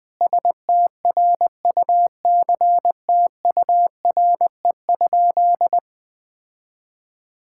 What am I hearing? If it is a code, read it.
Morse code, heard as a keyed tone, about 20 wpm, 700 Hz. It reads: STRUCTURE?
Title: Sentences from Top 1000 Words - Review Part 6 - 20wpm